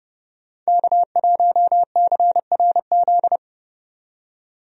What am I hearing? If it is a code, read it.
Morse code, heard as a keyed tone, about 31 wpm, 700 Hz. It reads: K1CRZ
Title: USA Call Signs - 5 Characters - 30wpm